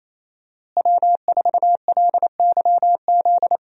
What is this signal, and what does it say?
Morse code, heard as a keyed tone, about 28 wpm, 700 Hz. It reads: W4LYZ
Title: USA Call Signs - 5 Characters - Repeated 3x - 28wpm